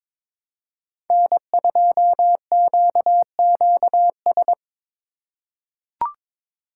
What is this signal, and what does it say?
Morse code, heard as a keyed tone, about 21 wpm, 700 Hz. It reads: N2QQS
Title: USA Call Signs - 5 Characters - 2x Character Spacing - 22wpm